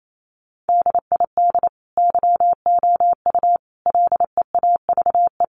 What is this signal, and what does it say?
Morse code, heard as a keyed tone, about 28 wpm, 700 Hz. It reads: DID YOU LEAVE
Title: Sentences from Top 1000 Words 28wpm